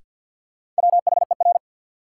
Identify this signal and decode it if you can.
Morse code, heard as a keyed tone, about 53 wpm, 700 Hz. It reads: WHER
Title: Sentences from Top 1500 Words - New Word Drill - 50wpm